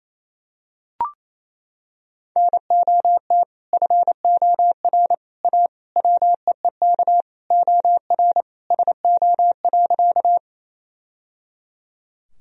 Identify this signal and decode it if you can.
Morse code, heard as a keyed tone, about 28 wpm, 700 Hz. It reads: NOT FOR A WEEK OR SO.
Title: Sentences from Top 1000 Words - Review Part 2 - 28wpm